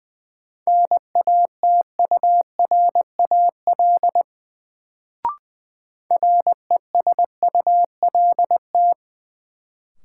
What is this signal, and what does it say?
Morse code, heard as a keyed tone, about 20 wpm, 700 Hz. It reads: NATURAL RESULT
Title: Sentences from Top 700 Words - New Word Drill - 20wpm